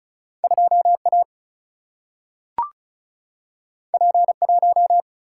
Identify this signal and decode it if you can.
Morse code, heard as a keyed tone, about 36 wpm, 700 Hz. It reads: 2 A P 1
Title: Single-Letter Single-Number 35wpm